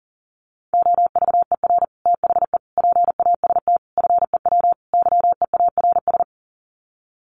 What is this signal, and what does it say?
Morse code, heard as a keyed tone, about 40 wpm, 700 Hz. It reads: OVER THE PAST FEW YEARS
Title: Sets of 5 Words Encore 40wpm